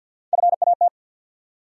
Morse code, heard as a keyed tone, about 54 wpm, 700 Hz. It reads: FAT
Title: USA Call Sign Postfixes - 3 Letters 50wpm